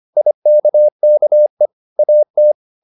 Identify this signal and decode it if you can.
Morse code, heard as a keyed tone, about 25 wpm, 600 Hz. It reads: IKKE AT
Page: Ferie 24 wpm